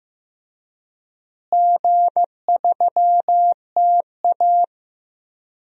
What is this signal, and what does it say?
Morse code, heard as a keyed tone, about 15 wpm, 700 Hz. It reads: G3TA
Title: DXCC-100 Call Signs - 4 Characters - Repeated 3x - 15wpm